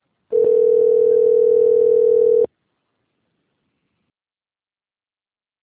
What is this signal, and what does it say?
Ringback tone (North America), 440 + 480 Hz.